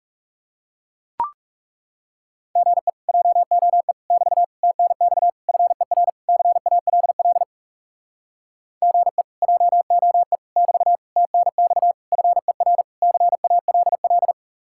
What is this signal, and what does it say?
Morse code, heard as a keyed tone, about 40 wpm, 700 Hz. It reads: GE JOE <BT> TNX FER CALL GE JOE <BT> TNX FER CALL